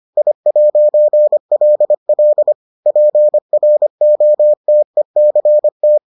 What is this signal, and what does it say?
Morse code, heard as a keyed tone, about 25 wpm, 600 Hz. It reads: I'LL PROTECT